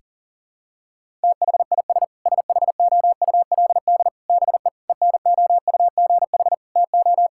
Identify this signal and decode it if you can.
Morse code, heard as a keyed tone, about 40 wpm, 700 Hz. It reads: THIS SHOULD BE ENOUGH TO
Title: Sentences from Top 1000 Words - Review Part 6 - 40wpm